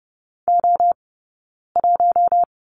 Morse code, heard as a keyed tone, about 30 wpm, 700 Hz. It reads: O 1
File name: Lesson 20d - Number 3 - Two Characters 30wpm